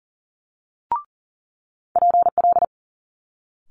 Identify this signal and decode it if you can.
Morse code, heard as a keyed tone, about 41 wpm, 700 Hz. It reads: PL
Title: Multi-letter Phonemes Quick Fire 40wpm